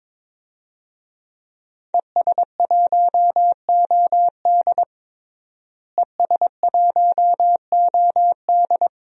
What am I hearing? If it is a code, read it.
Morse code, heard as a keyed tone, about 22 wpm, 700 Hz. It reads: ES1OD ES1OD
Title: DXCC-100 Call Signs - 5 Characters - Repeated 2x - 22wpm